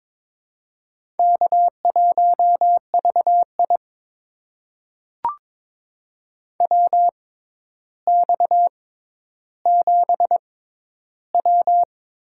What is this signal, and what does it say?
Morse code, heard as a keyed tone, about 22 wpm, spaced as for 11 wpm, 700 Hz. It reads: K1VI WX7W
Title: USA Call Signs - 4 Characters - 6x Character Spacing - 22wpm